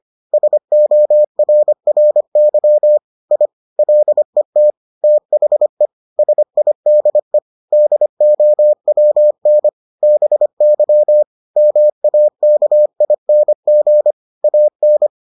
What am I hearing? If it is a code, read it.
Morse code, heard as a keyed tone, about 25 wpm, 600 Hz. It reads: SORRY I LET THE SIDE DOWN BY MAKING AN